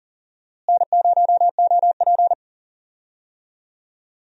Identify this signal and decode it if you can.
Morse code, heard as a keyed tone, about 40 wpm, 700 Hz. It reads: N0OP
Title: USA Call Signs - 4 Characters - Repeated 2x - 40wpm